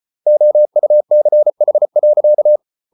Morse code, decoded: OUCH.

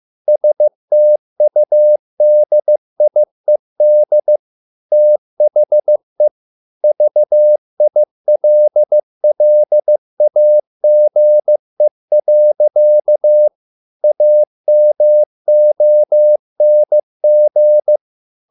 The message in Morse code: STUDIED THE VILLAGE. AMONG

